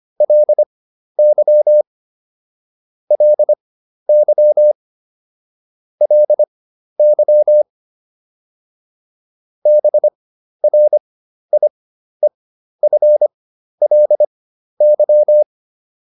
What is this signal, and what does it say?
Morse code, heard as a keyed tone, about 25 wpm, spaced as for 12 wpm, 600 Hz. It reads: LY LY LY BRIEFLY